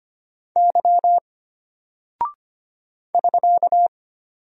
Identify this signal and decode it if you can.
Morse code, heard as a keyed tone, about 25 wpm, 700 Hz. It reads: Y <SK>
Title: Single Character (Letter:Number:Punctuation:Prosign) Alt 25wpm